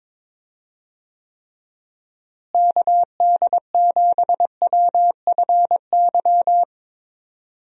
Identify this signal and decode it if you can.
Morse code, heard as a keyed tone, about 21 wpm, 700 Hz. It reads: KD7WFY
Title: USA Call Signs - 6 Characters - Repeated 2x - 22wpm